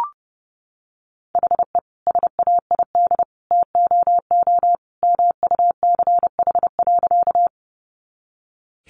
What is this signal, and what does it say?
Morse code, heard as a keyed tone, about 30 wpm, 700 Hz. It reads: HE SAID TOO MUCH.
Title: Sentences from Top 200 Words 30wpm